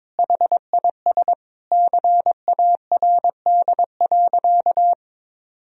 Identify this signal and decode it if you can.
Morse code, heard as a keyed tone, about 22 wpm, 700 Hz. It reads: HIS CARD.
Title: Sentences from Top 1000 Words - Review Part 4 - 22wpm